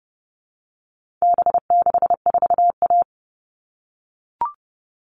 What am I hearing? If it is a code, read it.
Morse code, heard as a keyed tone, about 30 wpm, 700 Hz. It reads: B64A
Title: All Cut Numbers - 4-Digit 30wpm